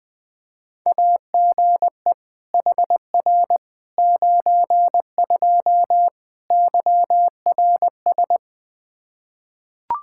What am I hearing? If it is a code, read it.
Morse code, heard as a keyed tone, about 21 wpm, 700 Hz. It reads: AGE HR 92 YRS
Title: K1USN SST QSO Practice 20wpm